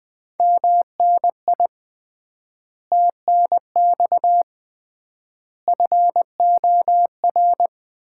Morse code, heard as a keyed tone, about 20 wpm, 700 Hz. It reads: MNI TNX FOR
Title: Common QSO Phrases - 3x Word-Spacing - 20wpm